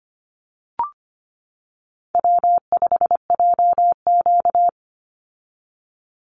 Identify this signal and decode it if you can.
Morse code, heard as a keyed tone, about 25 wpm, 700 Hz. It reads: W5JQ
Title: Call Signs - 25wpm